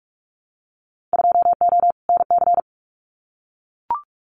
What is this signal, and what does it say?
Morse code, heard as a keyed tone, about 44 wpm, 700 Hz. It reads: 2O NC
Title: Winter Field Day Exchanges 45wpm